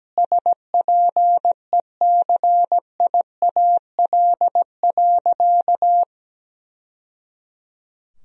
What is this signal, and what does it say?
Morse code, heard as a keyed tone, about 17 wpm, 700 Hz. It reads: SPECIAL.
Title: Sentences from Top 1000 Words - Review Part 3 - 17wpm